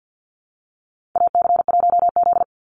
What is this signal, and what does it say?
Morse code, heard as a keyed tone, about 50 wpm, 700 Hz. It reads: AC1Z
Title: NA SOTA Chaser Callsigns 50wpm